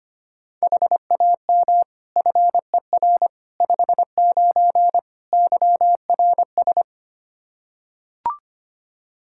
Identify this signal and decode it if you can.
Morse code, heard as a keyed tone, about 25 wpm, 700 Hz. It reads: HAM FER 59 YRS
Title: Practice QSO - Exchange 3 - More Info - 25wpm